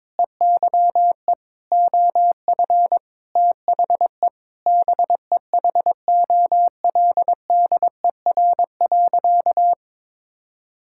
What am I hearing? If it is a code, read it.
Morse code, heard as a keyed tone, about 22 wpm, 700 Hz. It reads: EYE OF THE BEHOLDER.
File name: Sentences with 7-Word Phrases - 22wpm